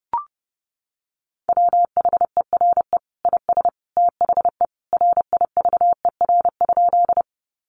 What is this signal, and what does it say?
Morse code, heard as a keyed tone, about 30 wpm, 700 Hz. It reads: WHERE IS THE RIVER?
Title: Sentences from Top 300 Words 30wpm